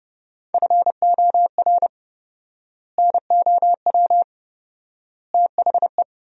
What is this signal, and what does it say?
Morse code, heard as a keyed tone, about 32 wpm, 700 Hz. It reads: FOR NOW THE
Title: 4x Word-Spacing - Sentences from Top 200 Words - 30wpm